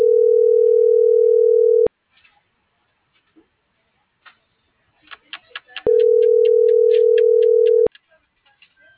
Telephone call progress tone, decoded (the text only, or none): ringback tone